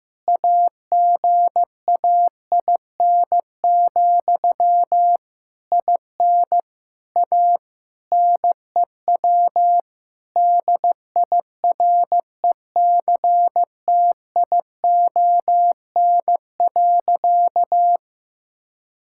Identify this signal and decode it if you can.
Morse code, heard as a keyed tone, about 15 wpm, 700 Hz. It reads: AGAIN, IN A NEW DIRECTION.
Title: Sentences from Top 1000 Words - Review Part 5 - 15wpm